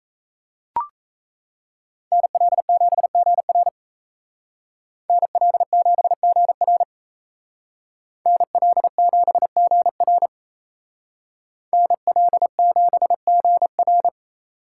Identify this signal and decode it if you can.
Morse code, heard as a keyed tone, about 33 wpm, 700 Hz. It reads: NL7GR NL7GR NL7GR NL7GR
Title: Speed Racing - USA 2x2 Call Signs 28wpm